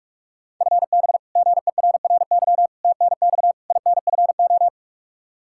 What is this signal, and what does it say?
Morse code, heard as a keyed tone, about 42 wpm, 700 Hz. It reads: FB GERRY TNX INFO
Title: Practice QSO - Exchange 3 - More Info - 45wpm